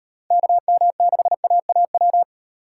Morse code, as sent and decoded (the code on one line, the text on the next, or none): -.- -- -.... .- .- .--
KM6AAW